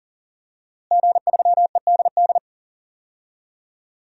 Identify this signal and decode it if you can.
Morse code, heard as a keyed tone, about 40 wpm, 700 Hz. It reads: G3EDD
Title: DXCC-100 Call Signs - 5 Characters - Repeated 3x - 40wpm